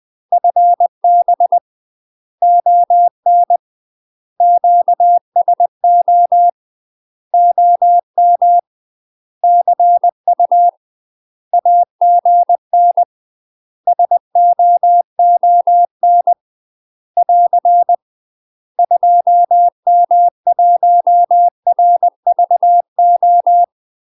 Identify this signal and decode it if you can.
Morse code, heard as a keyed tone, about 20 wpm, 700 Hz. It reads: FB ON QSO OM CU AGN SOON <AR> 2M1RVO
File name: UK MASTER QSOs TRAINING / 20WPM / UK MASTER QSOs 0053